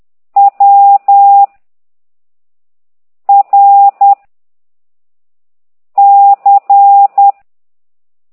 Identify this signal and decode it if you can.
Morse code, heard as a keyed tone, about 10 wpm, 800 Hz. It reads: W R C
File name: ct2idw / exerc9